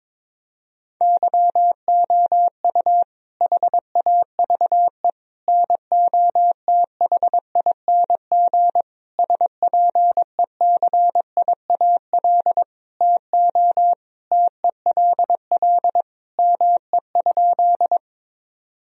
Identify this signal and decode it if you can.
Morse code, heard as a keyed tone, about 22 wpm, 700 Hz. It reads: YOU HAVE NOTHING SPECIAL TO TELL ME?
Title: Sentences from Top 1000 Words - Review Part 3 - 22wpm